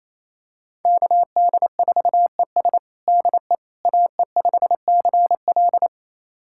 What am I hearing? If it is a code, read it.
Morse code, heard as a keyed tone, about 27 wpm, 700 Hz. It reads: KD4ES DE AE5CL
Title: USA Callsign DE Callsign - 5 Characters - 28wpm